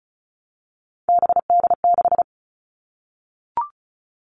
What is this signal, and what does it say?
Morse code, heard as a keyed tone, about 34 wpm, 700 Hz. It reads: BD6
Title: All Cut Numbers - 3-Digit 35wpm